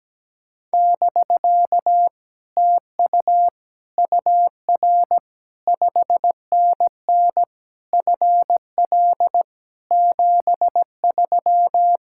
Morse code, heard as a keyed tone, about 17 wpm, 700 Hz. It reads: <BK> TU UR 5NN FL 73
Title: Speed-Racing Common QSO Phrases - 17wpm